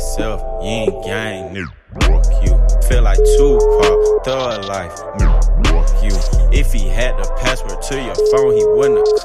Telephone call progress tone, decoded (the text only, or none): ringback tone